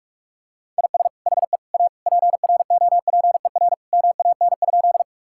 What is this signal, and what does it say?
Morse code, heard as a keyed tone, about 44 wpm, 700 Hz. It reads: IS HE A PROPER MAN?
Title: Sentences from Top 1000 Words - Review Part 5 - 45wpm